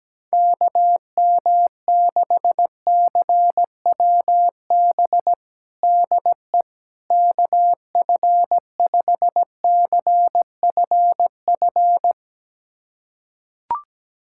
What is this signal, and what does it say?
Morse code, heard as a keyed tone, about 17 wpm, 700 Hz. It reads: KM6CWB DE KF5CFF
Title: USA Callsign DE Callsign 17wpm